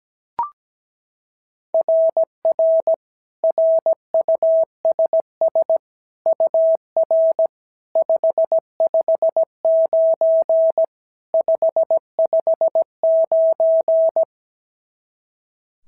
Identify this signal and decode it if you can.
Morse code, heard as a keyed tone, about 16 wpm, 650 Hz. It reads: RR RUSS UR 559 559
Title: SOTA QSO Practice 17wpm